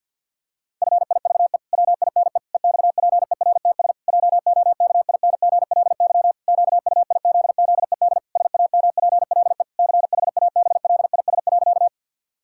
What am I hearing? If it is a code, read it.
Morse code, heard as a keyed tone, about 51 wpm, 700 Hz. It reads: FIVE WINE EXPERTS JOKINGLY QUIZZED SAMPLE CHABLIS.
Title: Pangram Challenge v2 - 50wpm